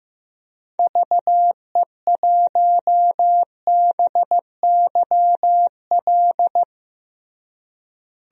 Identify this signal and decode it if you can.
Morse code, heard as a keyed tone, about 15 wpm, 700 Hz. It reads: VE1BYL
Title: DXCC-100 Call Signs - 6 Characters - Repeated 3x - 15wpm